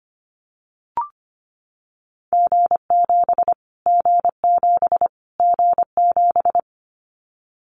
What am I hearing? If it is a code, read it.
Morse code, heard as a keyed tone, about 25 wpm, 700 Hz. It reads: G7 G7 G7
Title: Callsign Prefixes for Top 25 DXCC - Repeated 3x - 25wpm